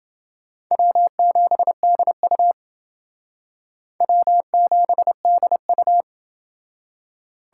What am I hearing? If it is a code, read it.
Morse code, heard as a keyed tone, about 28 wpm, 700 Hz. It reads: W7DU W7DU